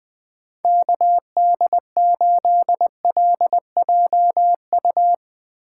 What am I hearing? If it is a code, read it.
Morse code, heard as a keyed tone, about 20 wpm, 700 Hz. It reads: KD8LJU
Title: USA Call Signs - 6 Characters - 8x Character Spacing - 20wpm